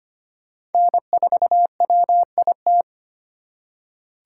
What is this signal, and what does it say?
Morse code, heard as a keyed tone, about 32 wpm, 700 Hz. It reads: N4WIT